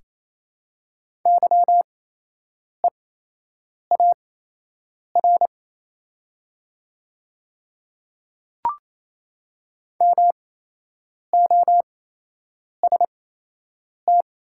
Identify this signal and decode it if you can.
Morse code, heard as a keyed tone, about 28 wpm, spaced as for 8 wpm, 690 Hz. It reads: YEAR MOST
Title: 8x Character Spacing - Top 300 Words - 28wpm